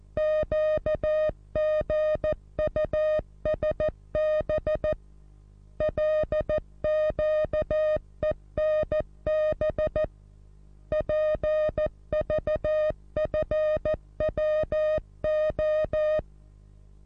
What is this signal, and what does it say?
Morse code, heard as a keyed tone, about 14 wpm, 620 Hz. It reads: QGUSB LQENB PVFWO
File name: repos / Morsekurs / DJ4UF Morsen27